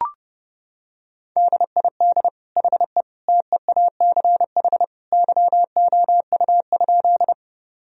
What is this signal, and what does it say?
Morse code, heard as a keyed tone, about 30 wpm, 700 Hz. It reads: DID HE TEACH YOU?
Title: Sentences from Top 1000 Words - Review Part 2 - 30wpm